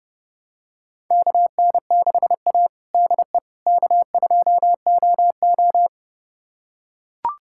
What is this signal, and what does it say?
Morse code, heard as a keyed tone, about 31 wpm, 700 Hz. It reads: KN6A DE K2OO
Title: USA Callsign DE Callsign - 4 Characters - 30wpm